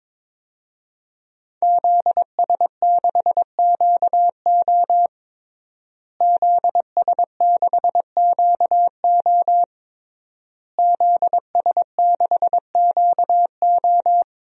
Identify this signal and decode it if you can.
Morse code, heard as a keyed tone, about 22 wpm, 700 Hz. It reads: ZS6QO ZS6QO ZS6QO
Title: DXCC-100 Call Signs - 5 Characters - Repeated 3x - 22wpm